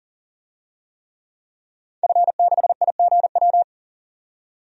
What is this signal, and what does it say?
Morse code, heard as a keyed tone, about 38 wpm, 700 Hz. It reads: F6IGW
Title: DXCC-100 Call Signs - 5 Characters - 40wpm